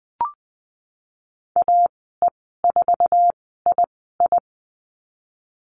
Morse code, heard as a keyed tone, about 19 wpm, spaced as for 14 wpm, 700 Hz. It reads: AE4II
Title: 048 Lesson 9g - Number 4 - Call Signs 20wpm